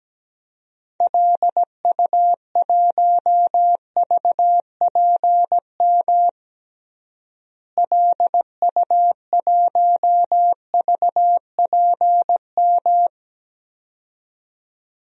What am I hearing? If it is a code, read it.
Morse code, heard as a keyed tone, about 17 wpm, 700 Hz. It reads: LU1VPM LU1VPM
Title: DXCC-100 Call Signs - 6 Characters - Repeated 2x - 17wpm